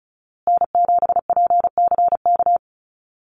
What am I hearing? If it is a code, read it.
Morse code, heard as a keyed tone, about 34 wpm, 700 Hz. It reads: N7PCK
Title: USA Call Signs Repeated 2x 35wpm